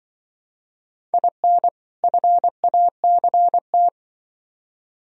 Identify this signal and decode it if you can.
Morse code, heard as a keyed tone, about 25 wpm, 700 Hz. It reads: IN FACT
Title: Speed-Racing - Sets of 2 Words 20wpm